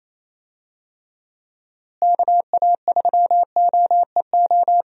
Morse code, heard as a keyed tone, about 27 wpm, 700 Hz. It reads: KA3OEO
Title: USA Call Signs - 6 Characters - Repeated 2x - 28wpm